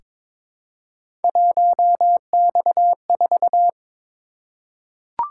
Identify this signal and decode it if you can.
Morse code, heard as a keyed tone, about 23 wpm, 700 Hz. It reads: 1X4
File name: practice / 22wpm / Three Letter-Number Combinations - 22wpm